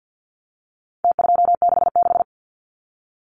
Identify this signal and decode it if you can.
Morse code, heard as a keyed tone, about 49 wpm, 700 Hz. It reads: T266
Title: Numbers - Contest Style - 4-Digit 50wpm